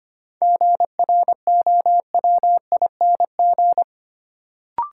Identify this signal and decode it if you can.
Morse code, heard as a keyed tone, about 25 wpm, 700 Hz. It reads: GROWING
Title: Sets of 5 Words Encore - New Word Drill - 25wpm